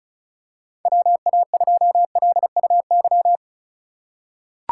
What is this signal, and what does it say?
Morse code, heard as a keyed tone, about 34 wpm, 700 Hz. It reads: WA2LUY
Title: USA Call Signs - 6 Characters - 2x Character Spacing - 35wpm